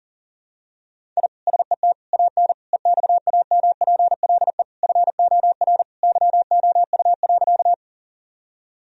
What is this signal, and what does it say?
Morse code, heard as a keyed tone, about 37 wpm, 700 Hz. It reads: I SET AN EXAMPLE FOR YOU.
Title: Sentences from Top 1000 Words - Review Part 1 - 40wpm